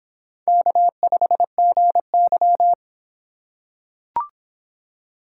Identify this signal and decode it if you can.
Morse code, heard as a keyed tone, about 23 wpm, 700 Hz. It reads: K5GY